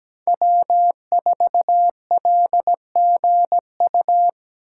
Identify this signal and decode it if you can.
Morse code, heard as a keyed tone, about 17 wpm, 700 Hz. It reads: W4LGU
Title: Code Groups - 5 Letter:Numbers - 17wpm